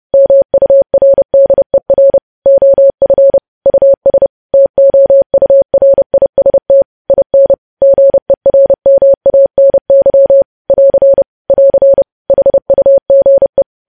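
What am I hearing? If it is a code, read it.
Morse code, heard as a keyed tone, about 30 wpm, 550 Hz. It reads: MURDER OF US TOURIST IN GERMANY <AR> <AR> HUGE
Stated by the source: Morse news headlines sent at 30 words per minute.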